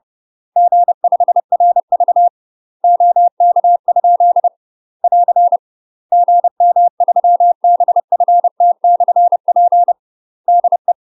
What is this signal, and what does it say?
Morse code, heard as a keyed tone, about 30 wpm, 700 Hz. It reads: G5RV OK? <AR> GM3BFT/P DE
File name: UK MASTER QSOs TRAINING / 30WPM / UK MASTER QSOs 0041